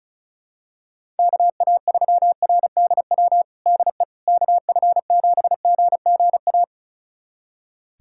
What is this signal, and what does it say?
Morse code, heard as a keyed tone, about 34 wpm, 700 Hz. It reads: KA3RDW DE KF7GGA
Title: USA Callsign DE Callsign 35wpm